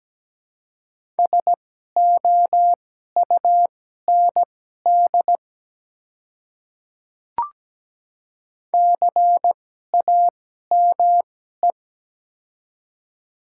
Morse code, heard as a keyed tone, about 17 wpm, spaced as for 12 wpm, 700 Hz. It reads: SOUND CAME
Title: 2x Character Spacing - Top 300 Words - 17wpm